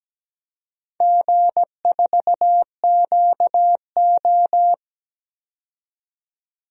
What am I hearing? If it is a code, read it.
Morse code, heard as a keyed tone, about 17 wpm, 700 Hz. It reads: G4QO
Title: DXCC-100 Call Signs - 4 Characters - 17wpm